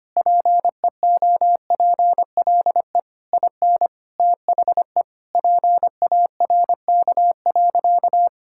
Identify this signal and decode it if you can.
Morse code, heard as a keyed tone, about 25 wpm, 700 Hz. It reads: PEOPLE IN THE PARK.
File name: practice / 25wpm / Sentences with 3-Word Phrases - 25wpm